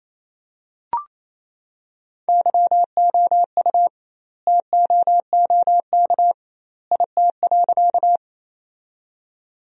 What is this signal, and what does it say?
Morse code, heard as a keyed tone, about 28 wpm, 700 Hz. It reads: YOU TOOK IT.
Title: 2x Word-Spacing - Sentences from Top 300 Words - 28wpm